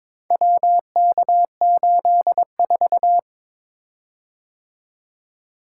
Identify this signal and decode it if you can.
Morse code, heard as a keyed tone, about 23 wpm, 700 Hz. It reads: WK84
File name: practice / 22wpm / Four Letter-Number Combinations - 22wpm